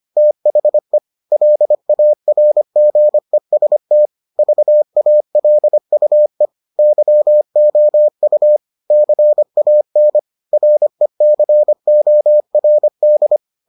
Morse code, heard as a keyed tone, about 25 wpm, 600 Hz. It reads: THE LARGEST VALUE YOU CAN RECORD